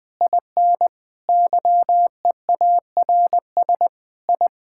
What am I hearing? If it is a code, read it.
Morse code, heard as a keyed tone, about 20 wpm, 700 Hz. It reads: IN YEARS I
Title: Sentences with 4-Word Phrases - Spoken Beginning - 20wpm